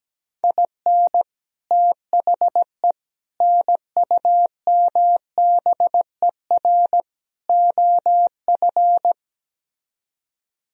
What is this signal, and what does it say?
Morse code, heard as a keyed tone, about 17 wpm, 700 Hz. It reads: IN THE NUMBER OF
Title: Speed-Racing - Sets of 4 Words 17wpm